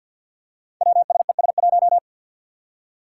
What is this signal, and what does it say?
Morse code, heard as a keyed tone, about 50 wpm, 700 Hz. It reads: WSES1
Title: Five Challenging Letter-Number Combinations 50wpm